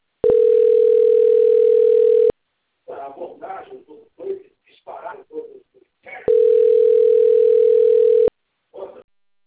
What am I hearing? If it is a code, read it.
Ringback tone (North America), 440 + 480 Hz, with speech.
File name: rg-1600-unknown- - - .62878